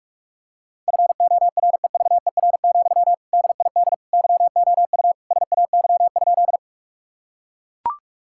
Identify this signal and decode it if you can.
Morse code, heard as a keyed tone, about 41 wpm, 700 Hz. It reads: FOREVER, DID YOU SAY?
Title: Sentences from Top 1600 Words 45wpm